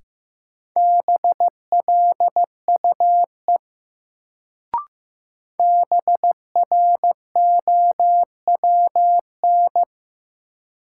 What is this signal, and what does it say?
Morse code, heard as a keyed tone, about 15 wpm, 700 Hz. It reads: BLUE BROWN
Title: Sentences from Top 1200 Words - New Word Drill Alt - 15wpm